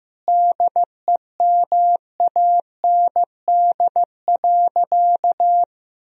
Morse code, heard as a keyed tone, about 15 wpm, 700 Hz. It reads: DEMAND.
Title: Sentences from Top 1000 Words - Review Part 4 - 15wpm